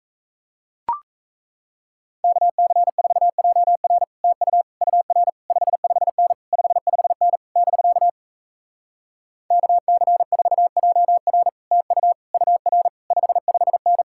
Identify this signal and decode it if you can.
Morse code, heard as a keyed tone, about 38 wpm, 700 Hz. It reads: KC4JR TU UR 55N 55N <BK> KC4JR TU UR 55N